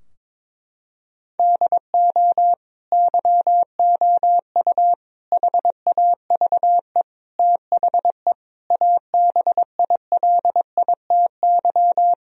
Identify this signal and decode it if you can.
Morse code, heard as a keyed tone, about 22 wpm, 700 Hz. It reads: DO YOU HAVE THE ABILITY
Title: Sentences from Top 1000 Words - Review Part 3 - 22wpm